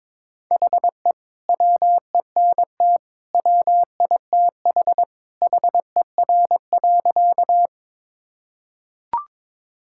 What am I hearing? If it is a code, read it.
Morse code, heard as a keyed tone, about 22 wpm, 700 Hz. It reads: HE WENT WITH HER.
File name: practice / 22wpm / Sentences with 2-Word Phrases - 22wpm